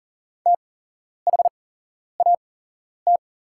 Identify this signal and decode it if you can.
Morse code, heard as a keyed tone, about 38 wpm, spaced as for 11 wpm, 700 Hz. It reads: THAT
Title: 8x Character Spacing - Top 300 Words - 40wpm